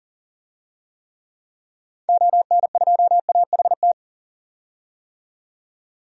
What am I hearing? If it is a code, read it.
Morse code, heard as a keyed tone, about 40 wpm, 700 Hz. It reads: ON2AHT
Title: DXCC-100 Call Signs - 6 Characters - 40wpm